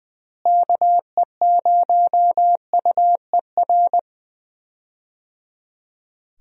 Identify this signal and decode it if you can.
Morse code, heard as a keyed tone, about 20 wpm, 700 Hz. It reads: KE0UER
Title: USA Call Signs - 6 Characters - Repeated 3x - 20wpm